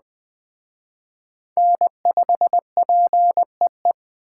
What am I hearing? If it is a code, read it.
Morse code, heard as a keyed tone, about 20 wpm, 700 Hz. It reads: N5PEE
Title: Speed Racing - USA 1x3 Call Signs 17wpm